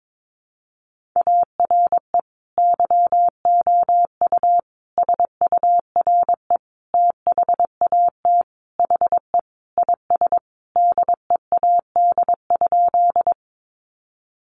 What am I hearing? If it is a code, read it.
Morse code, heard as a keyed tone, about 22 wpm, 700 Hz. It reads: ARE YOU SURE THAT HE IS DEAD?
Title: Sentences from Top 1500 Words 22wpm